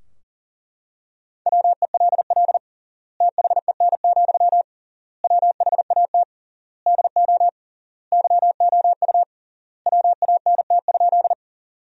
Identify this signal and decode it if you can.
Morse code, heard as a keyed tone, about 40 wpm, 700 Hz. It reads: WELL THEN, WHAT DO YOU WANT?
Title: 3x Word-Spacing - Sentences from Top 200 Words - 40wpm